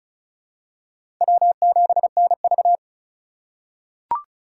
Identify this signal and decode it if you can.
Morse code, heard as a keyed tone, about 36 wpm, 700 Hz. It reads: W7NV
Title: USA Call Signs - 3x Character Spacing - 35wpm